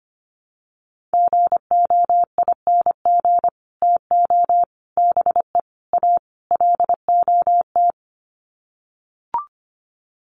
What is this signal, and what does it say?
Morse code, heard as a keyed tone, about 25 wpm, 700 Hz. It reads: GOING TO BE A LOT
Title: Sets of 5 Words Encore 25wpm